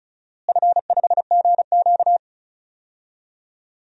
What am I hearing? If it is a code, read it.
Morse code, heard as a keyed tone, about 36 wpm, 700 Hz. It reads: F5GQ
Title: DXCC-100 Call Signs - 4 Characters - Repeated 3x - 35wpm